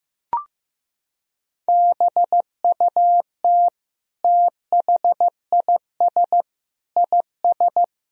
Morse code, heard as a keyed tone, about 15 wpm, 700 Hz. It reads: BUT THIS IS
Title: Sentences from Top 1000 Words - Review Part 3 - 15wpm